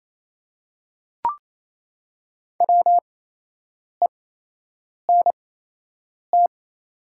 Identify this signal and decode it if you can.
Morse code, heard as a keyed tone, about 28 wpm, spaced as for 8 wpm, 700 Hz. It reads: WENT
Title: 8x Character Spacing - Top 500 Words - 28wpm